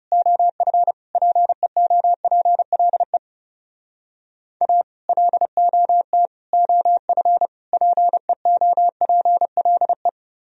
Morse code, heard as a keyed tone, about 35 wpm, 700 Hz. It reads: OF PEOPLE A LOT OF PEOPLE
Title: Speed-Racing - Sets of 4 Words 30wpm